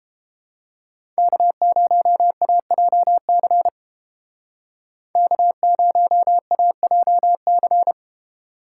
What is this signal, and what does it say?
Morse code, heard as a keyed tone, about 29 wpm, 700 Hz. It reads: K0AJC K0AJC